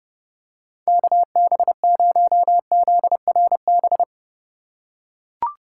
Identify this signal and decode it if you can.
Morse code, heard as a keyed tone, about 31 wpm, 700 Hz. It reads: KB0ZRB
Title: USA Call Signs - 6 Characters - 3x Character Spacing - 30wpm